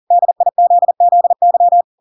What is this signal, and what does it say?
Morse code, heard as a keyed tone, about 40 wpm, 700 Hz. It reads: DIZZY
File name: Pinocchio / Pinocchio ch5 40 40 ik1hge